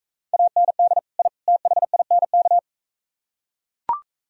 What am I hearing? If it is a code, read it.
Morse code, heard as a keyed tone, about 44 wpm, 700 Hz. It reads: AND I THINK